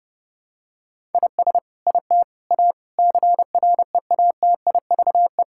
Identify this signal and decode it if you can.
Morse code, heard as a keyed tone, about 30 wpm, 700 Hz. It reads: IS IT A CREATIVE
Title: Sentences from Top 1000 Words - Review Part 5 - 30wpm